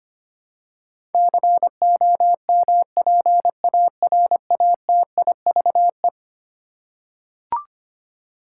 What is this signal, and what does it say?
Morse code, heard as a keyed tone, about 25 wpm, 700 Hz. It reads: COMPARATIVE
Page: morse-code-ninja-practice-25wpm